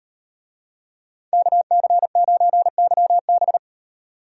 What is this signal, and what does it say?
Morse code, heard as a keyed tone, about 38 wpm, 700 Hz. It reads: KC9YB